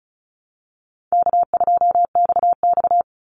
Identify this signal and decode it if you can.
Morse code, heard as a keyed tone, about 34 wpm, 700 Hz. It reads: K2XX
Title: Easy Call Signs Repeated 2x - 35wpm